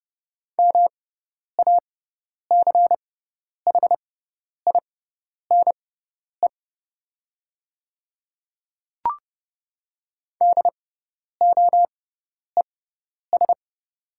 Morse code, heard as a keyed tone, about 30 wpm, spaced as for 10 wpm, 700 Hz. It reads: MACHINE DOES
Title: 6x Character Spacing - Top 500 Words - 30wpm